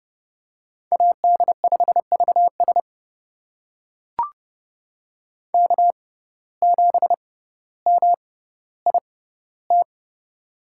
Morse code, heard as a keyed tone, about 31 wpm, spaced as for 16 wpm, 700 Hz. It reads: AD5VS K7MIT
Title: USA Call Signs - 5 Characters - 6x Character Spacing - 30wpm